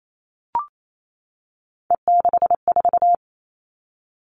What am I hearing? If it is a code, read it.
Morse code, heard as a keyed tone, about 28 wpm, 700 Hz. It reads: E64
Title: Three Challenging Letter-Number Combinations 28wpm